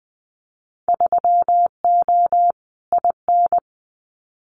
Morse code, heard as a keyed tone, about 20 wpm, 700 Hz. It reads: 3O IN
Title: Winter Field Day Exchanges Encore 20wpm